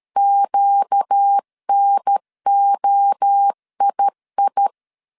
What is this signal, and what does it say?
Morse code, heard as a keyed tone, about 13 wpm, 800 Hz. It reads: QNOII
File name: CW / DET / 1 6 1500C S B 12wpm 3431